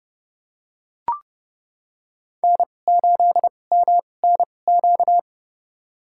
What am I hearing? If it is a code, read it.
Morse code, heard as a keyed tone, about 30 wpm, spaced as for 22 wpm, 700 Hz. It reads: N8MNQ
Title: USA Call Signs - 5 Characters - 2x Character Spacing - 30wpm